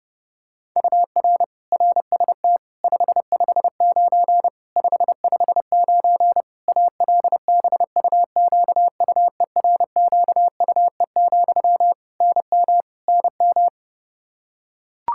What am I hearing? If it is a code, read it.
Morse code, heard as a keyed tone, about 32 wpm, 700 Hz. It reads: UR RST 559 559 ALBUQUERQUE, NM NM
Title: Practice QSO - Exchange 1 - Beginning - 30wpm